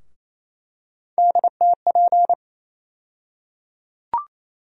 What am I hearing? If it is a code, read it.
Morse code, heard as a keyed tone, about 28 wpm, 700 Hz. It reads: DTP
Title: USA Call Sign Postfixes - 3 Letters 28wpm